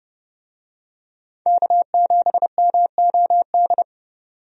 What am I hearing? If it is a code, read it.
Morse code, heard as a keyed tone, about 30 wpm, 700 Hz. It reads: K7MOD
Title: USA Call Signs - 5 Characters - 30wpm